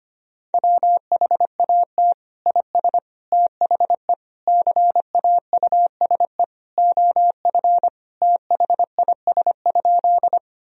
Morse code, heard as a keyed tone, about 25 wpm, 700 Hz. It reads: WHAT IS THE CAUSE OF THIS?
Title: Sentences from Top 1000 Words - Review Part 1 - 25wpm